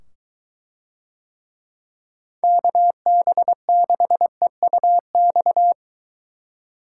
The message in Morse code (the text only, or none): KB6EUX